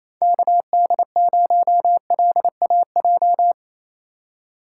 Morse code, heard as a keyed tone, about 28 wpm, 700 Hz. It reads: KD0LAJ
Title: USA Call Signs - 6 Characters - 2x Character Spacing - 28wpm